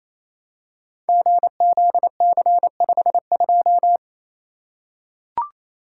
Morse code, heard as a keyed tone, about 28 wpm, 700 Hz. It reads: GZC52
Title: Code Groups - 5 Letter:Numbers - 28wpm